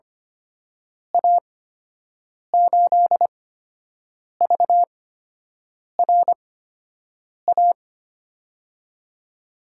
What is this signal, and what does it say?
Morse code, heard as a keyed tone, about 25 wpm, spaced as for 7 wpm, 700 Hz. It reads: A8VRA
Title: USA Call Signs - 6 Characters - 8x Character Spacing - 25wpm